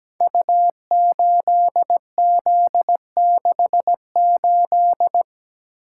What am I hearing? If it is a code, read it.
Morse code, heard as a keyed tone, about 17 wpm, 700 Hz. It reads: U8Z68
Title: Code Groups - 5 Letter:Numbers - 17wpm